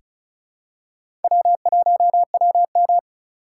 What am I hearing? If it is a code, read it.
Morse code, heard as a keyed tone, about 36 wpm, 700 Hz. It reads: W1WM
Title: USA Call Signs - 4 Characters - 4x Character Spacing - 35wpm